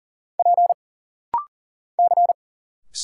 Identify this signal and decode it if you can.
Morse code, heard as a keyed tone, about 40 wpm, 700 Hz. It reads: P C
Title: Single Letters - Rapid-Fire 40wpm